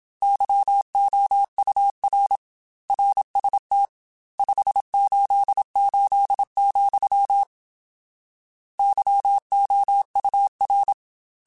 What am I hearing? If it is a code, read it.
Morse code, heard as a keyed tone, about 27 wpm, 800 Hz. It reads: YOUR RST 588, YOUR
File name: RSTs / RST-mp3s / Your RSTs @25